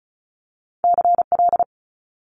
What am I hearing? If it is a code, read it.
Morse code, heard as a keyed tone, about 36 wpm, 700 Hz. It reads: CL
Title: Multi-letter Phonemes Quick Fire 35wpm